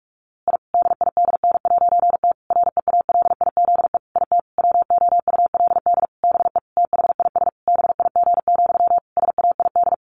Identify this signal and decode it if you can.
Morse code, heard as a keyed tone, about 44 wpm, 700 Hz. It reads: I DIDN'T REALIZE IT WOULD BE THIS BIG, SAID
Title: A Sound Of Thunder 45wpm